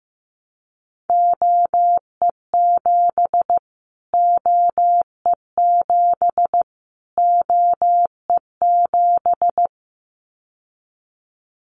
Morse code, heard as a keyed tone, about 15 wpm, 700 Hz. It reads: OE7 OE7 OE7
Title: Callsign Prefixes for Top 25 DXCC - Repeated 3x - 15wpm